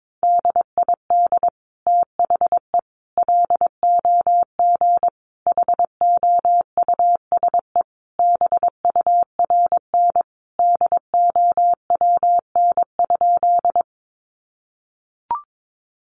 Morse code, heard as a keyed tone, about 22 wpm, 700 Hz. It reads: DID THE LOG HOUSE BURN DOWN?
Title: Sentences from Top 1400 Words 22wpm